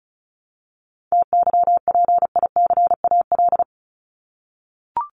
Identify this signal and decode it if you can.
Morse code, heard as a keyed tone, about 34 wpm, 700 Hz. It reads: TYPICAL
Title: LY Suffix - Top 100 Words with Base Words 35wpm